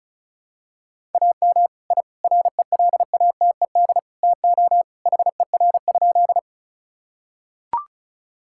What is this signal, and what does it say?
Morse code, heard as a keyed tone, about 34 wpm, 700 Hz. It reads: AM I RELATED TO HER?
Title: Sentences from Top 1000 Words - Review Part 5 - 35wpm